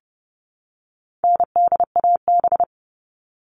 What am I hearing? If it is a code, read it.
Morse code, heard as a keyed tone, about 30 wpm, 700 Hz. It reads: NDAB
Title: All Cut Numbers - 4-Digit 30wpm